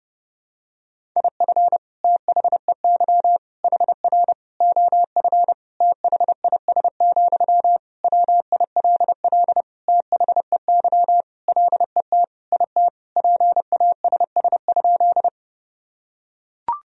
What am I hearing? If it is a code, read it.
Morse code, heard as a keyed tone, about 30 wpm, 700 Hz. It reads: IF THEY HR OF THIS, WILL THEY LET IT PASS?
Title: Sentences from Top 400 Words and Abbreviations 30wpm